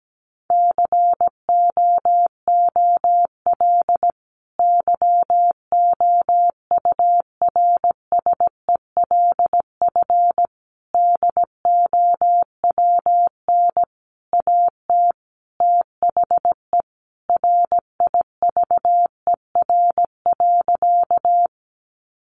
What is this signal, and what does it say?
Morse code, heard as a keyed tone, about 17 wpm, 700 Hz. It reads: COOL YOURSELF DOWN AT THE RIVER.
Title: Sentences from Top 1200 Words 17wpm